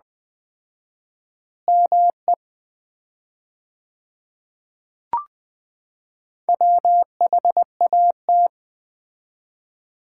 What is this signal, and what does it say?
Morse code, heard as a keyed tone, about 20 wpm, 700 Hz. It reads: ME WHAT
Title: 4x Word-Spacing - Sets of 4 Words - 20wpm